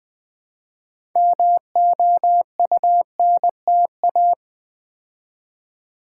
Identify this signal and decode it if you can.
Morse code, heard as a keyed tone, about 20 wpm, 700 Hz. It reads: MOUNTA
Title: Build A Word - 3 Letters At A Time - 8-Letter Words - 20wpm